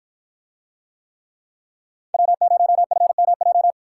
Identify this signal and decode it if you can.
Morse code, heard as a keyed tone, about 53 wpm, 700 Hz. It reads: W0FMJ